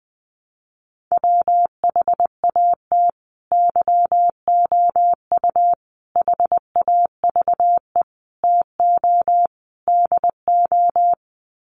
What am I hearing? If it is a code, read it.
Morse code, heard as a keyed tone, about 20 wpm, 700 Hz. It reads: WHAT YOU HAVE TO DO
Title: Sets of 5 Words Encore 20wpm